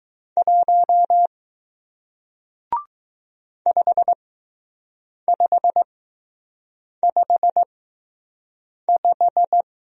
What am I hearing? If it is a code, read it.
Morse code, heard as a keyed tone, about 20 wpm, spaced as for 6 wpm, 700 Hz. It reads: 1 5555